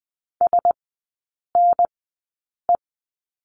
Morse code, heard as a keyed tone, about 20 wpm, spaced as for 8 wpm, 700 Hz. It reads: SNE
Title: 035 Lesson 7e - Letter s - Three Characters 20wpm